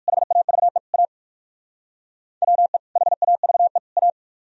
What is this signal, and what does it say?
Morse code, heard as a keyed tone, about 57 wpm, 700 Hz. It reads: HAVE A WE HAVE A